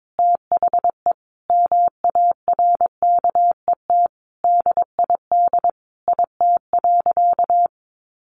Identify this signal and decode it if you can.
Morse code, heard as a keyed tone, about 22 wpm, 700 Hz. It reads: THE MARKET DID IT.
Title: Sentences from Top 600 Words 22wpm